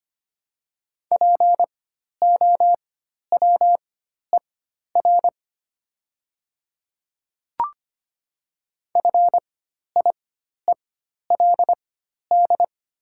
Morse code, heard as a keyed tone, about 25 wpm, spaced as for 12 wpm, 700 Hz. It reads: POWER FIELD
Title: 4x Character Spacing - Top 500 Words - 25wpm